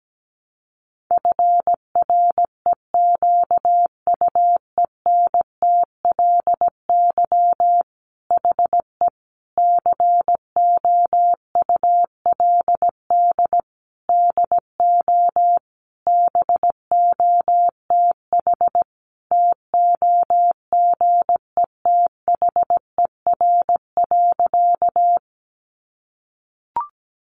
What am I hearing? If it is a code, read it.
Morse code, heard as a keyed tone, about 17 wpm, 700 Hz. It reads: FREQUENTLY HE COULD DO BOTH TOGETHER.
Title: Sentences from Top 1300 Words 17wpm